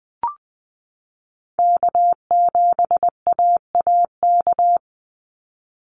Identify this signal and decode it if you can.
Morse code, heard as a keyed tone, about 20 wpm, 700 Hz. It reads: K7AAK
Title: USA Call Signs 20wpm